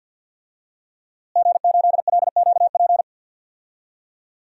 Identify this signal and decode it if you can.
Morse code, heard as a keyed tone, about 50 wpm, 700 Hz. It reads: G8LQP
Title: DXCC-100 Call Signs - 5 Characters - Repeated 2x - 50wpm